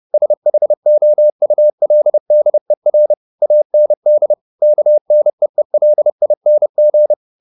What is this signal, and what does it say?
Morse code, heard as a keyed tone, about 30 wpm, 600 Hz. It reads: SHOULDER AND KNEELING
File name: the old man and the sea / the old man and the sea 1 30wpm